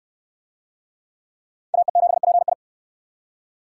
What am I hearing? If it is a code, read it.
Morse code, heard as a keyed tone, about 59 wpm, 700 Hz. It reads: AE7PI